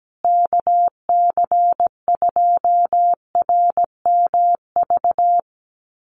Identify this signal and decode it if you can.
Morse code, heard as a keyed tone, about 17 wpm, 700 Hz. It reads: KC2RMV
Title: USA Call Signs Repeated 3x 17wpm